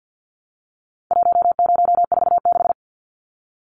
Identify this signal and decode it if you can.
Morse code, heard as a keyed tone, about 50 wpm, 700 Hz. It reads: 1046
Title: Numbers - 4-Digit 50wpm